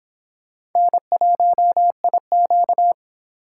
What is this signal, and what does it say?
Morse code, heard as a keyed tone, about 25 wpm, 700 Hz. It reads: N1IQ